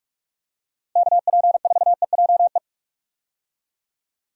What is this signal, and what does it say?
Morse code, heard as a keyed tone, about 44 wpm, 700 Hz. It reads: KP4EJE
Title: DXCC-100 Call Signs - 6 Characters - Repeated 3x - 45wpm